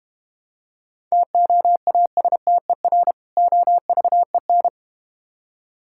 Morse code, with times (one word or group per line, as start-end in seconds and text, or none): 1.1-3.1 TOASTER
3.4-4.7 OVEN